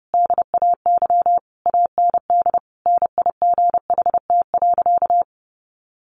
Morse code, heard as a keyed tone, about 30 wpm, 700 Hz. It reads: DAY AND NIGHT.
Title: Sentences from Top 1500 Words 30wpm